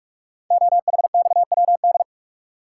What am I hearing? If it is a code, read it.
Morse code, heard as a keyed tone, about 44 wpm, 700 Hz. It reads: OHXWD
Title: Code Groups - 5 Letters - 45wpm